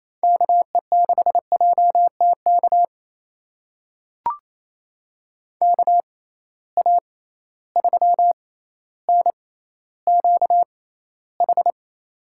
Morse code, heard as a keyed tone, about 28 wpm, spaced as for 14 wpm, 700 Hz. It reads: KE6JTK KA3NQH
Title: USA Call Signs - 6 Characters - 6x Character Spacing - 28wpm